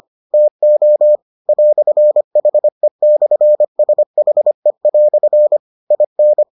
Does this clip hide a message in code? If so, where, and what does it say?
Yes: Morse code from 0.3 to 6.4 s, heard as a keyed tone, about 25 wpm, 600 Hz. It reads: TO "HE/SHE" IN